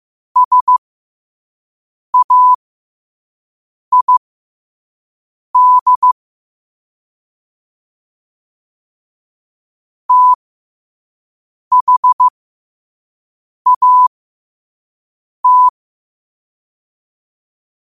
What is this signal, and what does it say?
Morse code, heard as a keyed tone, about 15 wpm, spaced as for 5 wpm, 1000 Hz. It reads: SAID THAT
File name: qotdfiles / 5wpm